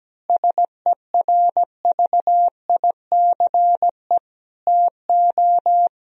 Morse code, heard as a keyed tone, about 17 wpm, 700 Hz. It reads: SERVICE TO
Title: Sentences from Top 1000 Words - Review Part 4 - 17wpm